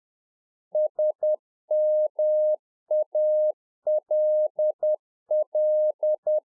Morse code, heard as a keyed tone, about 10 wpm, 610 Hz. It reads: SMALL